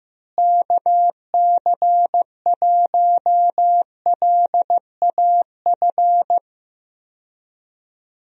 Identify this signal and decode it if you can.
Morse code, heard as a keyed tone, about 15 wpm, 700 Hz. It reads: KC1LAF
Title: USA Call Signs - 6 Characters - Repeated 3x - 15wpm